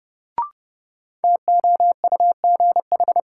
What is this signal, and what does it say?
Morse code, heard as a keyed tone, about 30 wpm, 700 Hz. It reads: TOUGH
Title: Sentences from Top 1200 Words - New Word Drill - 30wpm